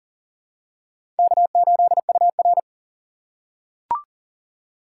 Morse code, heard as a keyed tone, about 40 wpm, 700 Hz. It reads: K8UR
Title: USA Call Signs - 4 Characters - 2x Character Spacing - 40wpm